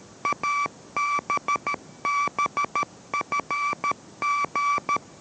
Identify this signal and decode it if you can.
Morse code, heard as a keyed tone, about 13 wpm, 1160 Hz. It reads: ABBFG